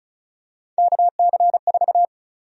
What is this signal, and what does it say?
Morse code, heard as a keyed tone, about 36 wpm, 700 Hz. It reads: KC4
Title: USA Call Sign Prefixes - 35wpm